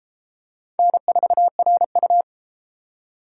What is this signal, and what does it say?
Morse code, heard as a keyed tone, about 31 wpm, 700 Hz. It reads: N4RU